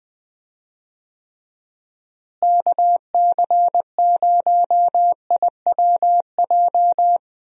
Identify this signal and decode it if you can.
Morse code, heard as a keyed tone, about 20 wpm, 700 Hz. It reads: KC0IWJ
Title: USA Call Signs - 6 Characters - Repeated 3x - 20wpm